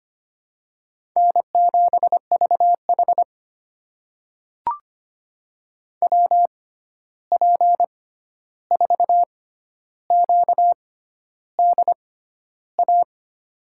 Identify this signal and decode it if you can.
Morse code, heard as a keyed tone, about 25 wpm, spaced as for 11 wpm, 700 Hz. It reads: N7VH WP4QDA
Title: USA Call Signs - 6x Character Spacing - 25wpm